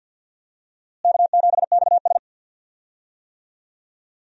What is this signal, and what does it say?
Morse code, heard as a keyed tone, about 52 wpm, 700 Hz. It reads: K7XS
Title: USA Call Signs - 4 Characters - Repeated 2x - 50wpm